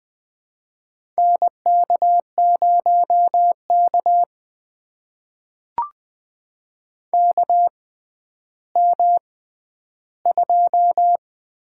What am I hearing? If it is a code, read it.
Morse code, heard as a keyed tone, about 20 wpm, spaced as for 11 wpm, 700 Hz. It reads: NK0K KM2
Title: USA Call Signs - 4 Characters - 6x Character Spacing - 20wpm